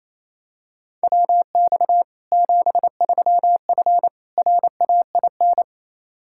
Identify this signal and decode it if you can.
Morse code, heard as a keyed tone, about 28 wpm, 700 Hz. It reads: WX 73F RAIN
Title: Practice QSO - Exchange 2 - Info - 28wpm